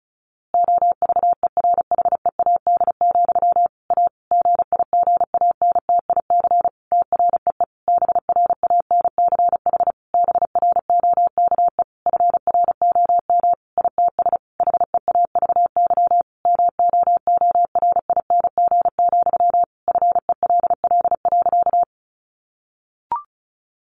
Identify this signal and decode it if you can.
Morse code, heard as a keyed tone, about 34 wpm, 700 Hz. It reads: OVERHEAD, A GIGANTIC TREE BRANCH BROKE FROM ITS HEAVY MOORING, FELL.
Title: A Sound Of Thunder 35wpm